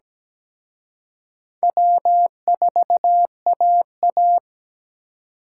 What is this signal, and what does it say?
Morse code, heard as a keyed tone, about 20 wpm, 700 Hz. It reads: W4AA
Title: Speed Racing - USA 1x2 Call Signs 17wpm